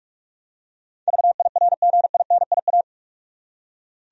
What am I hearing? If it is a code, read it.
Morse code, heard as a keyed tone, about 46 wpm, 700 Hz. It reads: VIRGINIA
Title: USA Common Names - 45wpm